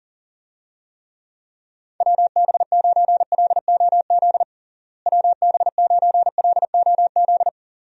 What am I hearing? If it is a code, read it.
Morse code, heard as a keyed tone, about 40 wpm, 700 Hz. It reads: WB9LOZ WB9LOZ
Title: USA Call Signs - 6 Characters - Repeated 2x - 40wpm